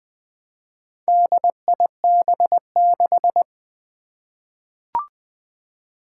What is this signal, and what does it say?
Morse code, heard as a keyed tone, about 20 wpm, 700 Hz. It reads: DIB6
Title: Four Challenging Letter-Number Combinations 20wpm